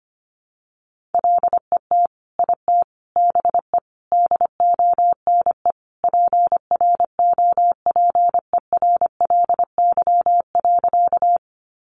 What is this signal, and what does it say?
Morse code, heard as a keyed tone, about 25 wpm, 700 Hz. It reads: LET IT BE DONE PROPERLY.
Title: Sentences from Top 1100 Words 25wpm